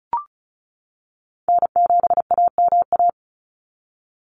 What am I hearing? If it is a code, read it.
Morse code, heard as a keyed tone, about 34 wpm, 700 Hz. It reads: N7AMA
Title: NA SOTA Chaser Callsigns 35wpm